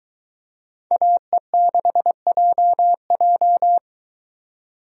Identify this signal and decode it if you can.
Morse code, heard as a keyed tone, about 23 wpm, 700 Hz. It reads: AE6JJ